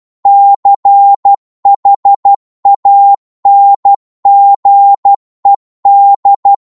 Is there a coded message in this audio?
Yes: Morse code from 0.3 to 6.5 s, heard as a keyed tone, about 12 wpm, 800 Hz. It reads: CHANGED